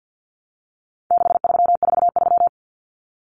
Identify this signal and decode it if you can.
Morse code, heard as a keyed tone, about 48 wpm, 700 Hz. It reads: 6343
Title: Common Cut Numbers - 4-Digit 50wpm